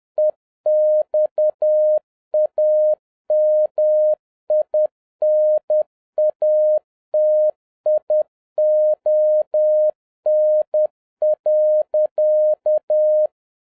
Morse code, heard as a keyed tone, about 10 wpm, 610 Hz. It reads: EXAMINATION.